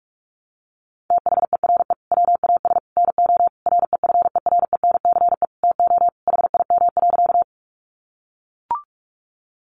Morse code, heard as a keyed tone, about 44 wpm, 700 Hz. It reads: THERE WAS NO REFERENCE TO HIM.
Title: Sentences from Top 1200 Words 45wpm